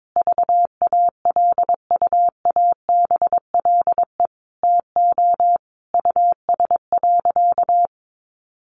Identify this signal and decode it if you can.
Morse code, heard as a keyed tone, about 22 wpm, 700 Hz. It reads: VALUABLE TO US.
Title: Sentences from Top 1100 Words 22wpm